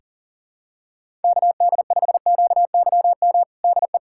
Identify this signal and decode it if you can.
Morse code, heard as a keyed tone, about 40 wpm, 700 Hz. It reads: KD5QYM DE
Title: USA Callsign DE Callsign - 6 Characters - 40wpm